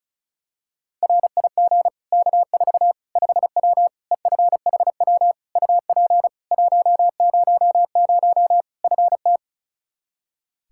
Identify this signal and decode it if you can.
Morse code, heard as a keyed tone, about 34 wpm, 700 Hz. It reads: RIG K4 5W EFHW UP 100 FT
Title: Practice QSO - Exchange 2 - Info - 35wpm